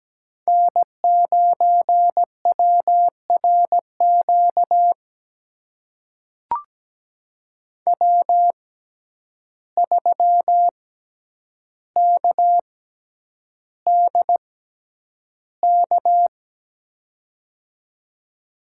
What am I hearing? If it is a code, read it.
Morse code, heard as a keyed tone, about 17 wpm, spaced as for 9 wpm, 700 Hz. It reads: N9WRQ W3KDK
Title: USA Call Signs - 5 Characters - 6x Character Spacing - 17wpm